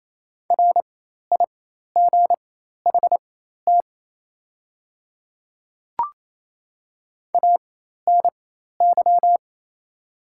Morse code, heard as a keyed tone, about 28 wpm, spaced as for 13 wpm, 700 Hz. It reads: RIGHT ANY
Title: 4x Character Spacing - Top 300 Words - 28wpm